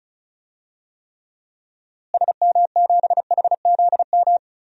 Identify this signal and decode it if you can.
Morse code, heard as a keyed tone, about 36 wpm, 700 Hz. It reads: SM7HZM
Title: DXCC-100 Call Signs - 6 Characters - Repeated 3x - 35wpm